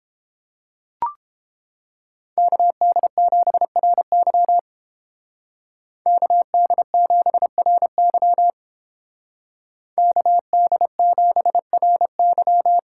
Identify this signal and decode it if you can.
Morse code, heard as a keyed tone, about 31 wpm, 700 Hz. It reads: KD7RY KD7RY KD7RY